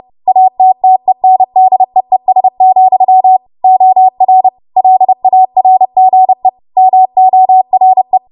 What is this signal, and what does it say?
Morse code, heard as a keyed tone, about 30 wpm, 750 Hz. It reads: ATTENDEES, OR LARGE MORE
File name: Morse / 30WPM